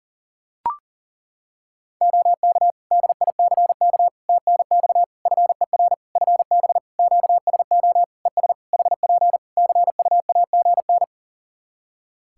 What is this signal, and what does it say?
Morse code, heard as a keyed tone, about 40 wpm, 700 Hz. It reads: OK DICK TNX FER FB QSO ES HP CUAGN
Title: Practice QSO - Exchange 4 - Ending - 40wpm